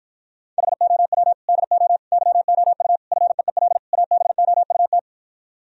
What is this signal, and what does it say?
Morse code, heard as a keyed tone, about 52 wpm, 700 Hz. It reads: HOW DO YOU FEEL ABOUT